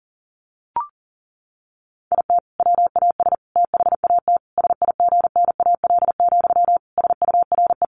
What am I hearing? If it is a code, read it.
Morse code, heard as a keyed tone, about 40 wpm, 700 Hz. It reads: IT WAS THAT SIGNAL, SURE
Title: Sentences from Top 1400 Words 40wpm